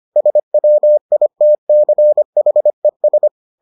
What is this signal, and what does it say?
Morse code, heard as a keyed tone, about 25 wpm, 600 Hz. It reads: SWITCHES